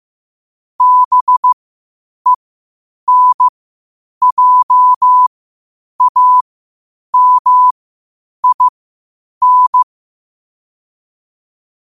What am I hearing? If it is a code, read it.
Morse code, heard as a keyed tone, about 15 wpm, spaced as for 8 wpm, 1000 Hz. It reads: BENJAMIN
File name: qotdfiles / 8wpm